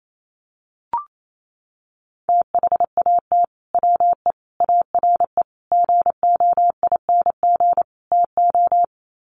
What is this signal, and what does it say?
Morse code, heard as a keyed tone, about 28 wpm, 690 Hz. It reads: THAT WE ARE GOING TO
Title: Sets of 5 Words Encore 28wpm